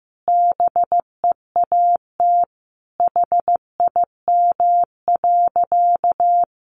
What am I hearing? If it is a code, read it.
Morse code, heard as a keyed tone, about 15 wpm, 700 Hz. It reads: BEAT HIM.
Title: Sentences from Top 1300 Words 15wpm